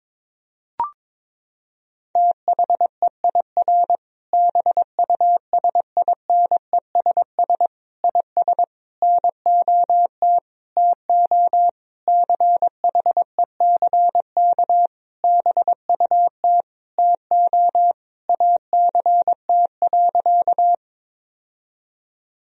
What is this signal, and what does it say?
Morse code, heard as a keyed tone, about 22 wpm, 700 Hz. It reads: THEIR BUSINESS IS NOT TO CHECK BUT TO ACT.
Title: Sentences from Top 1000 Words - Review Part 3 - 22wpm